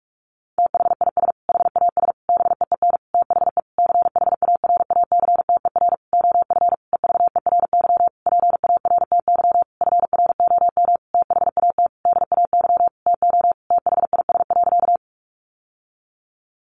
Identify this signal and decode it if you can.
Morse code, heard as a keyed tone, about 44 wpm, 700 Hz. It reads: THIS HAS BEEN THE CHARACTER OF EVERY PARTY FROM THAT DAY TO THIS.
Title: Sentences from Top 1400 Words 45wpm